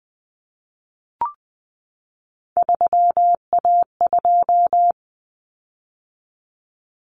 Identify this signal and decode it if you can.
Morse code, heard as a keyed tone, about 20 wpm, 700 Hz. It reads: 3A2
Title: Common Cut Numbers - 3-Digit 20wpm